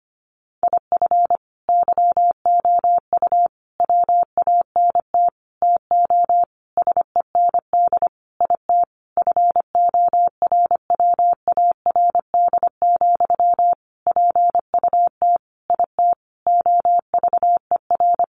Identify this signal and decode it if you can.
Morse code, heard as a keyed tone, about 25 wpm, 700 Hz. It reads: IF YOU WANT TO SEND IT FORWARD, PUT IT OVER
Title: Sentences from Top 1500 Words 25wpm